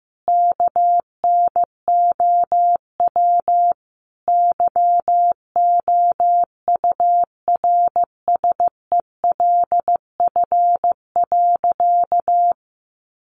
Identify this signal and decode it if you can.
Morse code, heard as a keyed tone, about 14 wpm, 700 Hz. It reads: KNOW YOURSELF.
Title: Sentences from Top 600 Words 15wpm